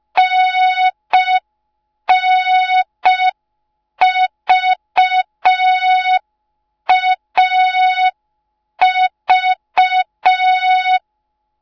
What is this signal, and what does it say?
Morse code, heard as a keyed tone, about 5 wpm, 3700 Hz. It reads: NNVAV